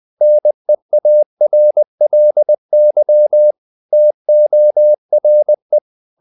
Morse code, heard as a keyed tone, about 20 wpm, 600 Hz. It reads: NEARLY TORE
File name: the old man and the sea / the old man and the sea part 1 20wpm